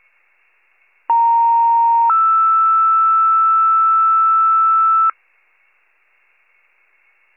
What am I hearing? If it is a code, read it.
Two-tone page, heard as a one-second tone, then a three-second tone, about 930 then 1320 Hz. It is heard from 1.1 to 5.1 s.